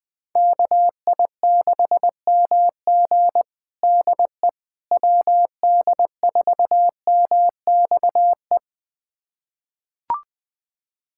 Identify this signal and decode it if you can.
Morse code, heard as a keyed tone, about 20 wpm, 700 Hz. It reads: KI6MG DE WD4MXE
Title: USA Callsign DE Callsign 20wpm